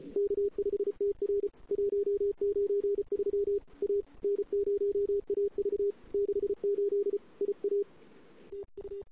Morse code, heard as a keyed tone, about 32 wpm, 410 Hz. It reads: K5TR 193 A N0AV 68 IA TU
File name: CW / 2008-ss-cw-k5tr-op-n5rz -02:00